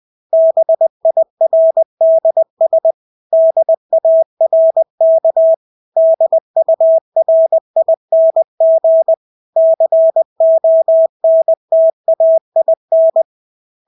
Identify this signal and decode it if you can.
Morse code, heard as a keyed tone, about 20 wpm, 650 Hz. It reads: BIRDS DARK DURING CONTAIN
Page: Engelske ord 20 wpm